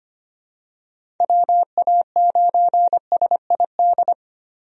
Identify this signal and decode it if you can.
Morse code, heard as a keyed tone, about 25 wpm, 700 Hz. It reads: WA9SID
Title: USA Call Signs - 6 Characters - Repeated 2x - 25wpm